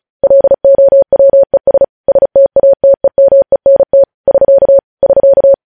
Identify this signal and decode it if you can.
Morse code, heard as a keyed tone, about 36 wpm, 550 Hz. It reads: LOWES STATEMENT <SK> <SK>
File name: podcast / News headlines-1x-35x35